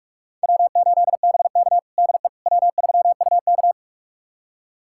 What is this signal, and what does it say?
Morse code, heard as a keyed tone, about 46 wpm, 700 Hz. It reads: W8BK DE W3UK
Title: USA Callsign DE Callsign - 4 Characters - 45wpm